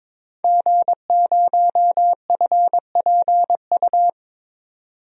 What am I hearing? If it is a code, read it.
Morse code, heard as a keyed tone, about 22 wpm, 700 Hz. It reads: G0FPU
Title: DXCC-100 Call Signs - 5 Characters - 22wpm